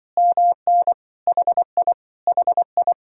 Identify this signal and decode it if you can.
Morse code, heard as a keyed tone, about 24 wpm, 700 Hz. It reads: MN HI HI
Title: Speed-Racing Common QSO Phrases - 20wpm